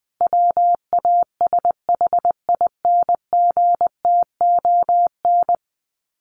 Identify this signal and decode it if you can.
Morse code, heard as a keyed tone, about 20 wpm, 700 Hz. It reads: WASHINGTON
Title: Sets of 5 Words Encore2 20wpm